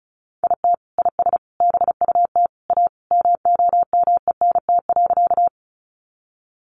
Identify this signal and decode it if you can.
Morse code, heard as a keyed tone, about 34 wpm, 700 Hz. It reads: IT IS BUT A MOMENT.
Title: Sentences from Top 1000 Words 35wpm